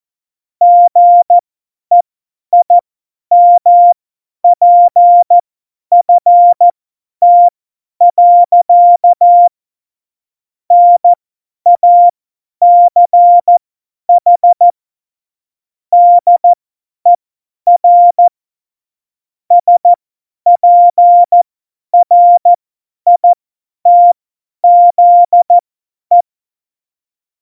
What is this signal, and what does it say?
Morse code, heard as a keyed tone, about 14 wpm, spaced as for 10 wpm, 700 Hz. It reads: GEIMPFT. NACH DER SPRITZE
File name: Morsekurs / Witze / Witz 05. Dezember 2017, 50 F70